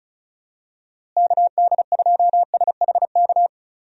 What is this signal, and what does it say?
Morse code, heard as a keyed tone, about 34 wpm, 700 Hz. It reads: KD2SHK
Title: USA Call Signs - 6 Characters - Repeated 3x - 35wpm